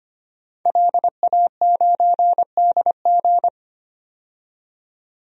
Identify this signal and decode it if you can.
Morse code, heard as a keyed tone, about 25 wpm, 700 Hz. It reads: LA9DG
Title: DXCC-100 Call Signs - 5 Characters - 25wpm